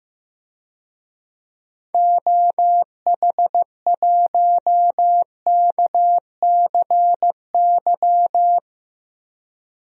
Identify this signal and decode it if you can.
Morse code, heard as a keyed tone, about 15 wpm, 700 Hz. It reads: OH1KCY
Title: DXCC-100 Call Signs - 6 Characters - Repeated 3x - 15wpm